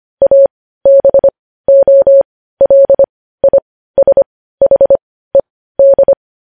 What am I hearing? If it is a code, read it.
Morse code, heard as a keyed tone, about 25 wpm, spaced as for 15 wpm, 550 Hz. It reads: ABOLISHED
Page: news headlines in morse code at 15 wpm